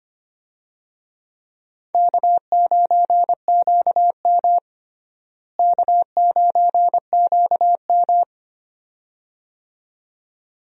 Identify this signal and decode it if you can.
Morse code, heard as a keyed tone, about 25 wpm, 700 Hz. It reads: K9QM K9QM
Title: USA Call Signs - 4 Characters - Repeated 2x - 25wpm